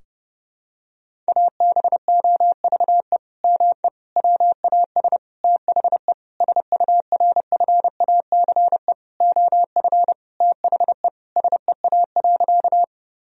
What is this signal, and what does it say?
Morse code, heard as a keyed tone, about 30 wpm, 700 Hz. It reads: ABOVE ME WAS THE SURFACE OF THE SEA.
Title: Sentences from Top 1000 Words - Review Part 3 - 30wpm